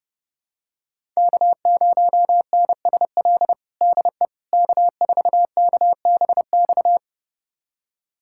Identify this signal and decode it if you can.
Morse code, heard as a keyed tone, about 32 wpm, 700 Hz. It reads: K0NSL DE K4KBX
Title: USA Callsign DE Callsign - 5 Characters - 30wpm